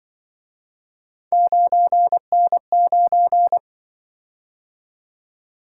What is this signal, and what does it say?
Morse code, heard as a keyed tone, about 24 wpm, 700 Hz. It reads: 9N9
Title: Speed-Racing - Three Letter-Number Combinations - 20wpm